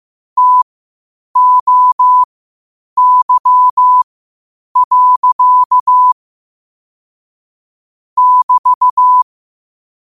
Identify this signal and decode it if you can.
Morse code, heard as a keyed tone, about 15 wpm, spaced as for 8 wpm, 1000 Hz. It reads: TOY. <BT>
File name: qotdfiles / 8wpm